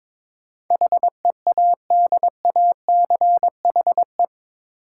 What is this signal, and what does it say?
Morse code, heard as a keyed tone, about 22 wpm, 700 Hz. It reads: HEADACHE
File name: practice / 22wpm / Eight Letter Words - 22wpm